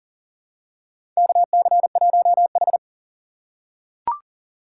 Morse code, heard as a keyed tone, about 40 wpm, 700 Hz. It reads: KC1H
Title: USA Call Signs - 3x Character Spacing - 40wpm